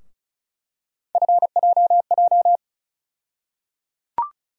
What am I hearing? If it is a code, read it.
Morse code, heard as a keyed tone, about 34 wpm, 700 Hz. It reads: FJJ
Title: USA Call Sign Postfixes - 3 Letters 35wpm